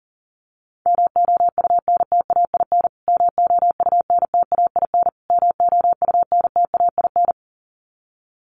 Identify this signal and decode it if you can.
Morse code, heard as a keyed tone, about 40 wpm, 700 Hz. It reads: MOUNTAIN MOUNTAIN MOUNTAIN
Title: Top 300 Words Repeated 40wpm